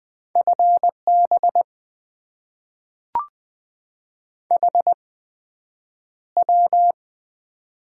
Morse code, heard as a keyed tone, about 20 wpm, spaced as for 9 wpm, 700 Hz. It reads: FB HW
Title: Top 20 QSO Elements - 8x Character Spacing - 20wpm